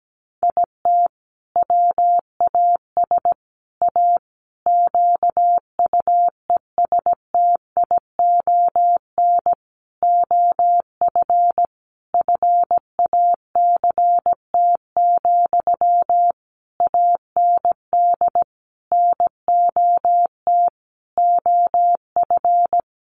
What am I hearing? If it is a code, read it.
Morse code, heard as a keyed tone, about 17 wpm, 700 Hz. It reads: IT WAS A QUESTION OF FACT, AND NOT OF
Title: Sentences from Top 1400 Words 17wpm